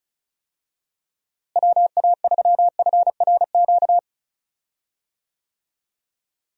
Morse code, heard as a keyed tone, about 36 wpm, 700 Hz. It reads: WA3FRQ
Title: USA Call Signs - 6 Characters - Repeated 2x - 35wpm